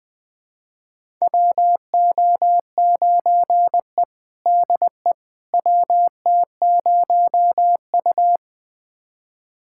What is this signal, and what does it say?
Morse code, heard as a keyed tone, about 20 wpm, 700 Hz. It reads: WO9E DE WT0U
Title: USA Callsign DE Callsign - 4 Characters - 20wpm